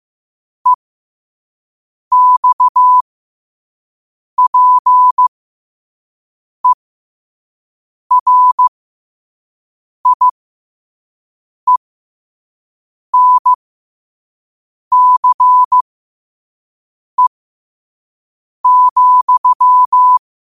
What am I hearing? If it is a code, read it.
Morse code, heard as a keyed tone, about 15 wpm, spaced as for 5 wpm, 1000 Hz. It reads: EXPERIENCE,
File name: qotdfiles / 5wpm